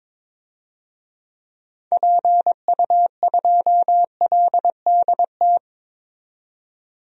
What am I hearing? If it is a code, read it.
Morse code, heard as a keyed tone, about 22 wpm, 700 Hz. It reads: PU2LDT
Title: DXCC-100 Call Signs - 6 Characters - 22wpm